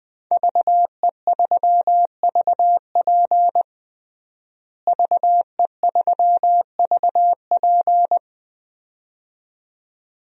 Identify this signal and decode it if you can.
Morse code, heard as a keyed tone, about 20 wpm, 700 Hz. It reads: VE3VP VE3VP
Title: DXCC-100 Call Signs - 5 Characters - Repeated 2x - 20wpm